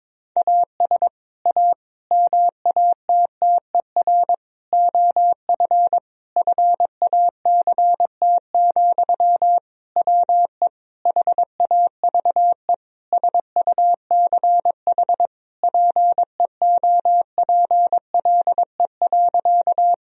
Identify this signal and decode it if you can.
Morse code, heard as a keyed tone, about 22 wpm, 700 Hz. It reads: AS A MATTER OF FACT, WE HAVE SUCH PEOPLE.
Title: Sentences from Top 1000 Words - Review Part 3 - 22wpm